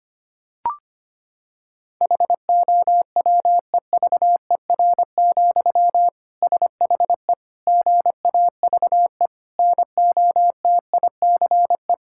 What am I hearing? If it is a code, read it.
Morse code, heard as a keyed tone, about 25 wpm, 700 Hz. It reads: HOWEVER, SHE GAVE NOTICE
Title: Sentences from Top 1000 Words - Review Part 6 - 25wpm